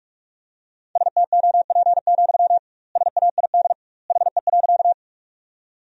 Morse code, heard as a keyed tone, about 46 wpm, 700 Hz. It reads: STOP, SAID HE.
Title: 2x Word-Spacing - Sentences from Top 300 Words - 45wpm